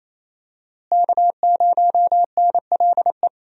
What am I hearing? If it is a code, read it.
Morse code, heard as a keyed tone, about 28 wpm, 700 Hz. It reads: K0NLE
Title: USA Call Signs - 5 Characters - Repeated 3x - 28wpm